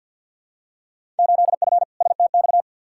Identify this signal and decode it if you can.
Morse code, heard as a keyed tone, about 47 wpm, 700 Hz. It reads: 8F STX
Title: Field Day Exchanges Encore - 50wpm